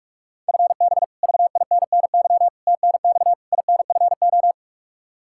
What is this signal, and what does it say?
Morse code, heard as a keyed tone, about 47 wpm, 700 Hz. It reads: FB VINNY TNX INFO
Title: Practice QSO - Exchange 3 - More Info - 45wpm